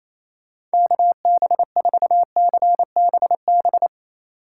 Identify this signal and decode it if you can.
Morse code, heard as a keyed tone, about 36 wpm, 700 Hz. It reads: KB4CBB